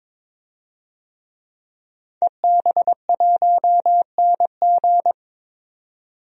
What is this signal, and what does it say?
Morse code, heard as a keyed tone, about 22 wpm, 700 Hz. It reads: EB1NG
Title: DXCC-100 Call Signs - 5 Characters - 22wpm